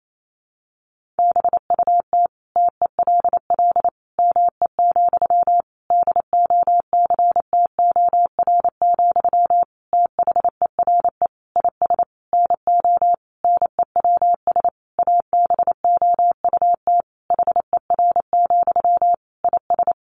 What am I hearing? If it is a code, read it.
Morse code, heard as a keyed tone, about 28 wpm, 700 Hz. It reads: BUT TELL ME, DOCTOR, THERE IS NO NEWS ABOUT HER, IS
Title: Sentences from Top 1500 Words 28wpm